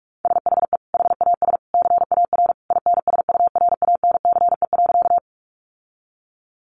Morse code, heard as a keyed tone, about 44 wpm, 700 Hz. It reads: SHE HAS CAR INSURANCE.
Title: Sentences from Top 1100 Words 45wpm